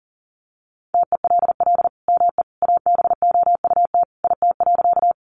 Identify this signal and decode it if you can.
Morse code, heard as a keyed tone, about 40 wpm, 700 Hz. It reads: TELL ME ABOUT IT.
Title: Sentences from Top 200 Words 40wpm